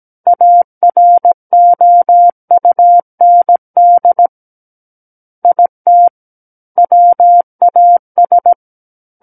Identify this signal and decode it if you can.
Morse code, heard as a keyed tone, about 17 wpm, 700 Hz. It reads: AROUND IT WAS
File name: Book One / 08-Chapter 8 - 17 WPM